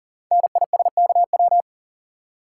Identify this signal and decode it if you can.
Morse code, heard as a keyed tone, about 41 wpm, 690 Hz. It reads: NISKW
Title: Code Groups - 5 Letters - 40wpm